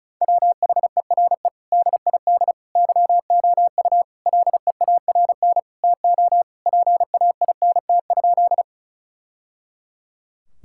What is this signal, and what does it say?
Morse code, heard as a keyed tone, about 34 wpm, 700 Hz. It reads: WHERE DID YOU LEARN TO PAINT?
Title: Sentences from Top 1000 Words - Review Part 6 - 35wpm